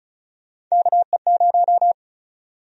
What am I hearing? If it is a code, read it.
Morse code, heard as a keyed tone, about 34 wpm, 700 Hz. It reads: KE0
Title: USA Call Sign Prefixes - 35wpm